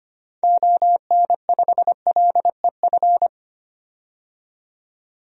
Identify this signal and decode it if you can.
Morse code, heard as a keyed tone, about 25 wpm, 700 Hz. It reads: ON5LEF
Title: DXCC-100 Call Signs - 6 Characters - 25wpm